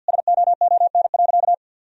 Morse code, heard as a keyed tone, about 50 wpm, 700 Hz. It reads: SOON.
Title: Sentences from Top 1000 Words - Review Part 4 - 50wpm